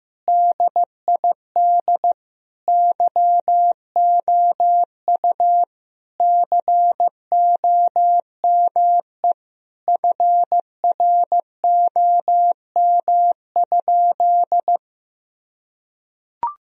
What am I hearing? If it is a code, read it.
Morse code, heard as a keyed tone, about 15 wpm, 700 Hz. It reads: DID YOU COME FROM?
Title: Sentences from Top 1000 Words - Review Part 2 - 15wpm